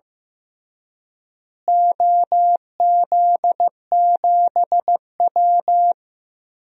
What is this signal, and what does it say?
Morse code, heard as a keyed tone, about 15 wpm, 700 Hz. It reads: OZ7W
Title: DXCC-100 Call Signs - 4 Characters - Repeated 3x - 15wpm